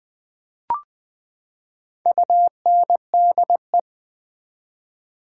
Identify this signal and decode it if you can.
Morse code, heard as a keyed tone, about 20 wpm, 700 Hz. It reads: UNDE
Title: Build A Word - 4 Letters At A Time - 10-Letter Words - 20wpm